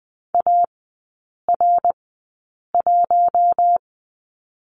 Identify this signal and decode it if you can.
Morse code, heard as a keyed tone, about 20 wpm, spaced as for 8 wpm, 700 Hz. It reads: AR1
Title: 053 Lesson 10e - Letter r - Three Characters 20wpm